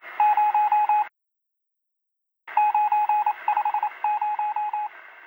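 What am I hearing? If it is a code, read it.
Morse code, heard as a keyed tone, about 28 wpm, 840 Hz. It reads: 0 950
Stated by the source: Morse from a game soundtrack, Google Earth VR.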